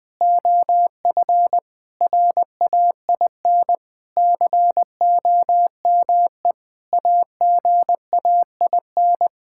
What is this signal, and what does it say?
Morse code, heard as a keyed tone, about 20 wpm, 700 Hz. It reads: OF RAIN COME AGAIN
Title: Sentences with 4-Word Phrases - 20wpm